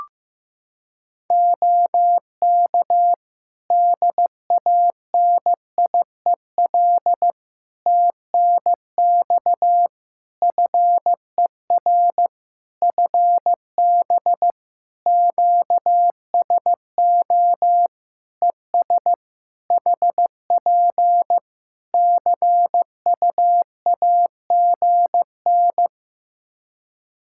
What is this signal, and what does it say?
Morse code, heard as a keyed tone, about 15 wpm, 700 Hz. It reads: OK DANIEL TNX FER FB QSO ES HP CUAGN
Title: Practice QSO - Exchange 4 - Ending - 15wpm